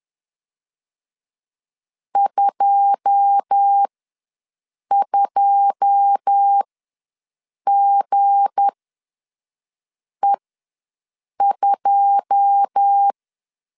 Morse code, heard as a keyed tone, about 10 wpm, spaced as for 5 wpm, 800 Hz. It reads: 22GE2
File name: LEZIONI / 9 2 ws 10wpm 4130